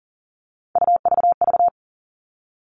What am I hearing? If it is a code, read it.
Morse code, heard as a keyed tone, about 38 wpm, 700 Hz. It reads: UVV
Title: Three Challenging Letter-Number Combinations 40wpm